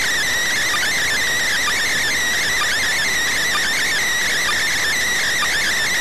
Emergency Alert System (SAME) tones (header bursts), heard as harsh buzzing bursts. It is heard from 0.7 to 5.2 s.